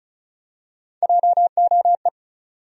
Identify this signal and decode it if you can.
Morse code, heard as a keyed tone, about 35 wpm, 700 Hz. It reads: JOE
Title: USA Common Names - Max 5 Letters - 35wpm